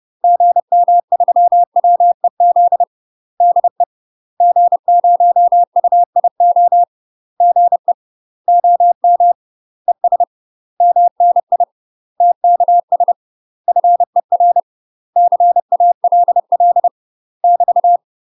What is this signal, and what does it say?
Morse code, heard as a keyed tone, about 30 wpm, 700 Hz. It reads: GM3WEZ DE G0UIO GE OM ES MNI TKS FER CALL <BT>
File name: UK MASTER QSOs TRAINING / 30WPM / UK MASTER QSOs 0030